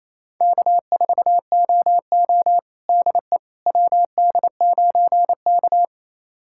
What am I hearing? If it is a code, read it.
Morse code, heard as a keyed tone, about 27 wpm, 700 Hz. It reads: K4OO DE WD9K
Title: USA Callsign DE Callsign - 4 Characters - 28wpm